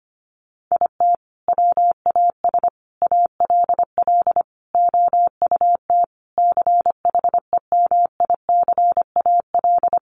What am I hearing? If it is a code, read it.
Morse code, heard as a keyed tone, about 25 wpm, 700 Hz. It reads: IT WAS ALL OUT CHEMICAL
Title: Sentences from Top 1200 Words 25wpm